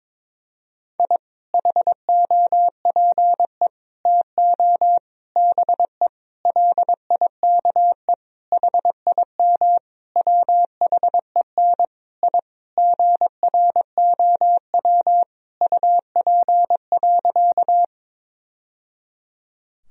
Morse code, heard as a keyed tone, about 22 wpm, 700 Hz. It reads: I HOPE TO BE LIKE HIM WHEN I GROW UP.
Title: Sentences from Top 1000 Words - Review Part 1 - 22wpm